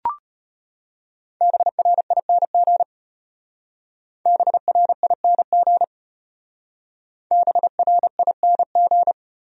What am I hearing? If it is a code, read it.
Morse code, heard as a keyed tone, about 32 wpm, 700 Hz. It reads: BRING BRING BRING